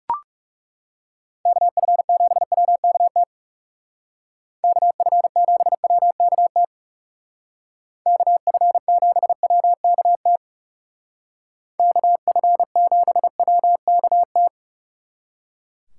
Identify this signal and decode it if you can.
Morse code, heard as a keyed tone, about 37 wpm, 700 Hz. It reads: KF7WKT KF7WKT KF7WKT KF7WKT
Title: Speed Racing - USA Call Signs 30wpm